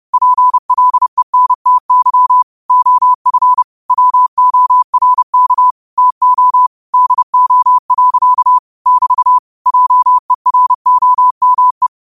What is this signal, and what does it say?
Morse code, heard as a keyed tone, about 30 wpm, 1000 Hz. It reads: PLENTY OF WORK TO DO. <BT> JEROME
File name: qotdfiles / 30wpm